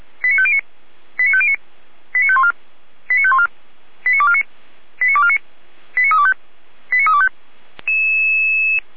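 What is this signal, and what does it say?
Five-tone selective call (ZVEI), heard as quick runs of five tones, tones of about 70 ms. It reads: 89509 89509 89514 89514 89249 89249 89325 89325
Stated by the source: ZVEI five-tone call received by SDR, from feuerwehr-orlamuende.org archive.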